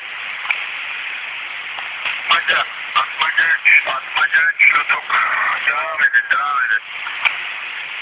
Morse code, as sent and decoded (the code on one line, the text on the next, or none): none